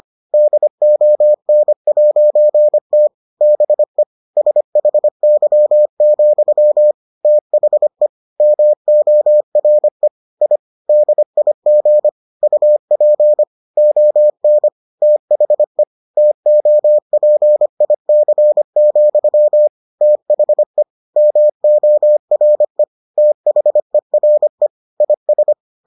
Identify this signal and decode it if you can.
Morse code, heard as a keyed tone, about 25 wpm, 600 Hz. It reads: DON'T BE SHY, THE MORE I DIG UP ON THE TOPIC, THE MORE THERE IS